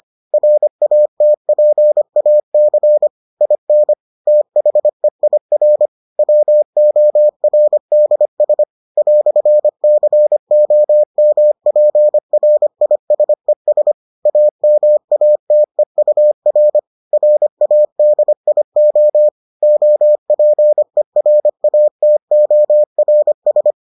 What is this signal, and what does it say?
Morse code, heard as a keyed tone, about 25 wpm, 600 Hz. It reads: RATPAC IN THEIR WORDS "COMPRISES AMATEUR RADIO OPERATORS